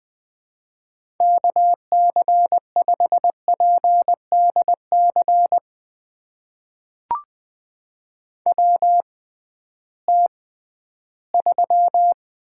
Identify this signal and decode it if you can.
Morse code, heard as a keyed tone, about 20 wpm, spaced as for 13 wpm, 700 Hz. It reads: KC5PDC WT3
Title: USA Call Signs - 6x Character Spacing - 20wpm